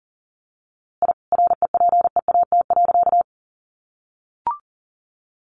Morse code, heard as a keyed tone, about 40 wpm, 700 Hz. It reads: I REPEAT.
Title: A Sound Of Thunder 40wpm